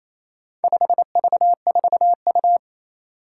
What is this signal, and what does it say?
Morse code, heard as a keyed tone, about 28 wpm, 700 Hz. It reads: 5V4U
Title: Four Challenging Letter-Number Combinations 28wpm